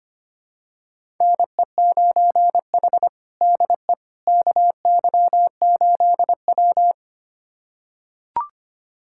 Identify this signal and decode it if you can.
Morse code, heard as a keyed tone, about 25 wpm, 700 Hz. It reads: NE9H DE KY8W
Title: USA Callsign DE Callsign - 4 Characters - 25wpm